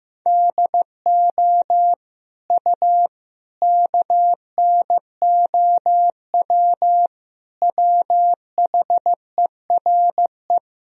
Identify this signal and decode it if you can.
Morse code, heard as a keyed tone, about 15 wpm, 700 Hz. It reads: DO U KNOW WHERE
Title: Sentences from Top 300 Words and Abbreviations 15wpm